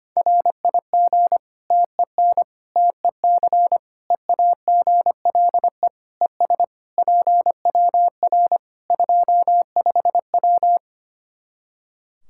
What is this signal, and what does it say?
Morse code, heard as a keyed tone, about 25 wpm, 700 Hz. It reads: RIG TEN TEC EAGLE ES PWR 25W
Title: Practice QSO - Exchange 2 - Info - 25wpm